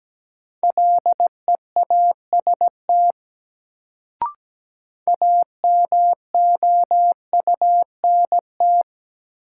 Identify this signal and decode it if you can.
Morse code, heard as a keyed tone, about 17 wpm, 700 Hz. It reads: LEAST AMOUNT
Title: Sentences from Top 700 Words - New Word Drill - 17wpm